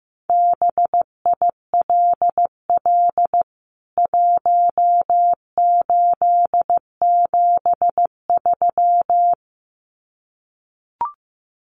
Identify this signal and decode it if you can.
Morse code, heard as a keyed tone, about 15 wpm, 700 Hz. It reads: BILL 1873
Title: CWT Exchange 15wpm